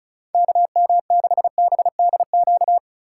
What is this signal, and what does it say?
Morse code, heard as a keyed tone, about 36 wpm, 700 Hz. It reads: KM6BDQ
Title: USA Call Signs - 6 Characters - 3x Character Spacing - 35wpm